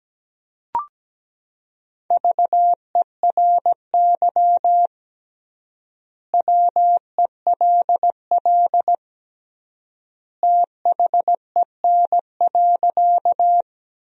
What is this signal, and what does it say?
Morse code, heard as a keyed tone, about 17 wpm, 700 Hz. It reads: VERY WELL THEN.
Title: 3x Word-Spacing - Sentences from Top 200 Words - 17wpm